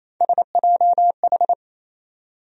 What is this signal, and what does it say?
Morse code, heard as a keyed tone, about 28 wpm, 700 Hz. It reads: SJH
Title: USA Call Sign Postfixes - 3 Letters 28wpm